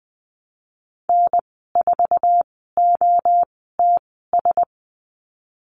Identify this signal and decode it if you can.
Morse code, heard as a keyed tone, about 20 wpm, spaced as for 15 wpm, 700 Hz. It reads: N4OTS
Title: 062 Lesson 11g - Letter h - Call Signs 20wpm